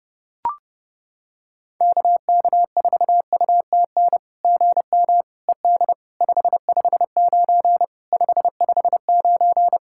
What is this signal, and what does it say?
Morse code, heard as a keyed tone, about 30 wpm, 690 Hz. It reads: KK4UTN GM ED 559 559
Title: SOTA QSO Practice 30wpm